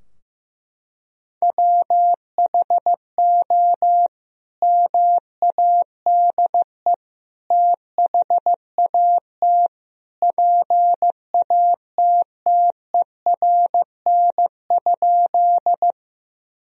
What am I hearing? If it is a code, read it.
Morse code, heard as a keyed tone, about 16 wpm, 700 Hz. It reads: WHO MADE THAT PATTERN?
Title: Sentences from Top 1000 Words - Review Part 2 - 15wpm